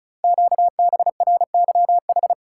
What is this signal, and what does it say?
Morse code, heard as a keyed tone, about 36 wpm, 690 Hz. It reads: QBRYH
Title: Code Groups - 5 Letters - 35wpm